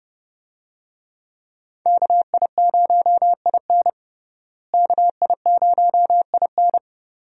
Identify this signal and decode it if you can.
Morse code, heard as a keyed tone, about 31 wpm, 700 Hz. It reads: KI0IN KI0IN
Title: USA Call Signs - 5 Characters - Repeated 2x - 30wpm